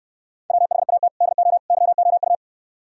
Morse code, heard as a keyed tone, about 67 wpm, 700 Hz. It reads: WHAT DO YOU